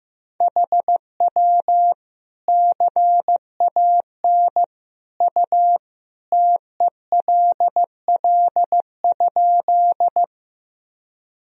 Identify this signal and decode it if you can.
Morse code, heard as a keyed tone, about 15 wpm, 700 Hz. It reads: HW CAN U TELL?
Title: Sentences from Top 200 Words and Abbreviations 15wpm